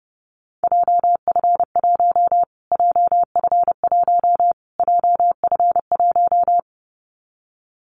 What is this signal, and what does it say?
Morse code, heard as a keyed tone, about 30 wpm, 700 Hz. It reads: JF1 JF1 JF1
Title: Callsign Prefixes for Top 25 DXCC - Repeated 3x - 30wpm